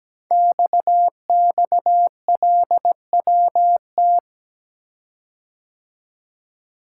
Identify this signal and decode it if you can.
Morse code, heard as a keyed tone, about 17 wpm, 700 Hz. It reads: XXLWT
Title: Code Groups - 5 Letters - 17wpm